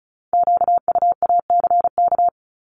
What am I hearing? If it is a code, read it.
Morse code, heard as a keyed tone, about 36 wpm, 700 Hz. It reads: QUACK
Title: Multi-letter Phonemes 35wpm